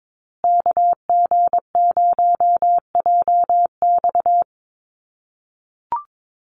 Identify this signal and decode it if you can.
Morse code, heard as a keyed tone, about 22 wpm, 700 Hz. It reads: KG0JX
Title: USA Call Signs Repeated 3x 22wpm